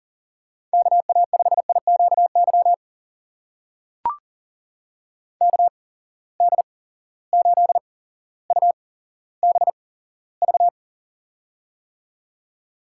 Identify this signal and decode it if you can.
Morse code, heard as a keyed tone, about 40 wpm, spaced as for 17 wpm, 700 Hz. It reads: KA5IQY KD8UBV
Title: USA Call Signs - 6 Characters - 8x Character Spacing - 40wpm